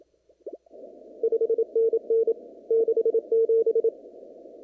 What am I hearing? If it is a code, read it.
Morse code, heard as a keyed tone, about 27 wpm, 450 Hz. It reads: 5NN 67
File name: CW / kt5j wpx cw-op-k5pi